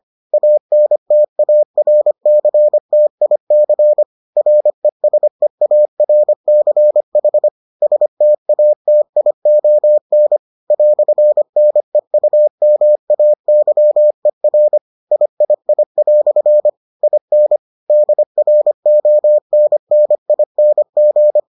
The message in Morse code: ANTARCTIC RESEARCH STATION "NEUMAYER III" IN DRONNING

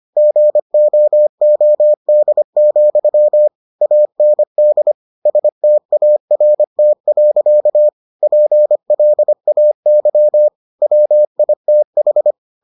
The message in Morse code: GOOD, AND START. PLAY WITH

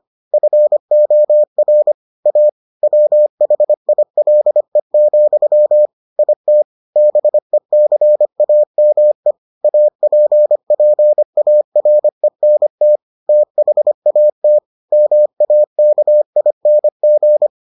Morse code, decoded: FOR A WHILE, IT BECAME APPARENT THAT MAKING

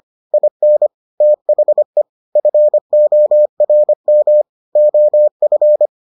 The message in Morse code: IN THE FORM OF

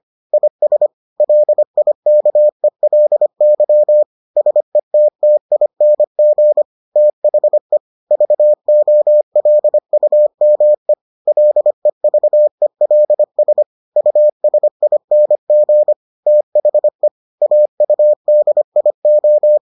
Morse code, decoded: IS LIKELY SETTING THE VOLUME LEVELS USING THE AUDIO

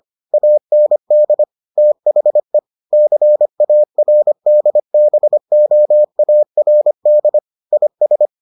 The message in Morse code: AND THE CARDBOARD IS

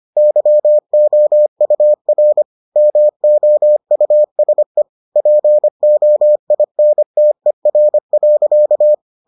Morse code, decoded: YOUR MOUSE POINTER.

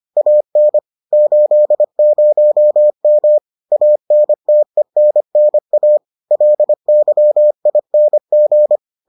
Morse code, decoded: AN 80M ANTENNA LYING

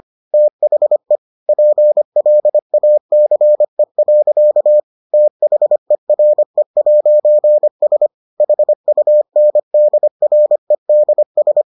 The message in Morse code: THE PLACE. THERE'S HUNDREDS

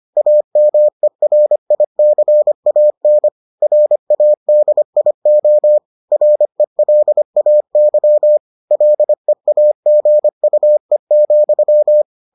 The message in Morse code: AMERICAN RADIO RELAY LEAGUE,

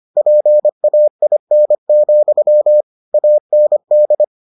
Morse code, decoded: PAIN, AND

